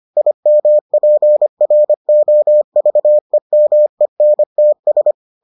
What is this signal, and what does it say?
Morse code, heard as a keyed tone, about 25 wpm, 600 Hz. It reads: IMPROVEMENTS